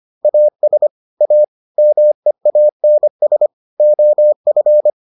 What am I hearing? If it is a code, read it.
Morse code, heard as a keyed tone, about 25 wpm, 600 Hz. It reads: AS A MEANS OF